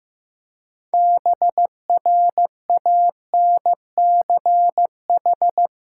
Morse code, heard as a keyed tone, about 16 wpm, 700 Hz. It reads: BRANCH
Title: Practice QSO - Exchange 3 - More Info - 15wpm